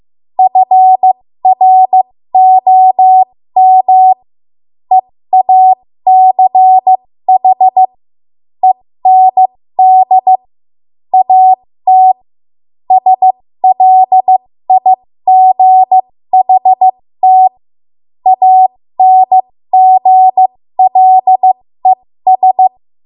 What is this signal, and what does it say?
Morse code, heard as a keyed tone, about 15 wpm, spaced as for 13 wpm, 750 Hz. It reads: FROM EACH END AT SLIGHT ANGLES
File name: Morse / 13WPM